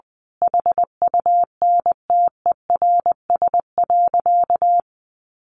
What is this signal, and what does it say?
Morse code, heard as a keyed tone, about 20 wpm, 700 Hz. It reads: HUNTERS.
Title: A Sound Of Thunder 20wpm